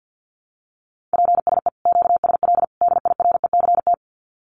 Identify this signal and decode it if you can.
Morse code, heard as a keyed tone, about 50 wpm, 700 Hz. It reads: PSE QSL DIRECT
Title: Common QSO Phrases 50wpm